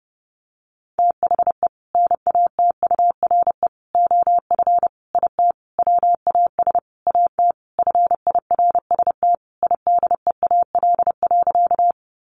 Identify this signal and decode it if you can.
Morse code, heard as a keyed tone, about 28 wpm, 700 Hz. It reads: THE NATURE OF IT WAS AT FIRST IDEAL.
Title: Sentences from Top 1400 Words 30wpm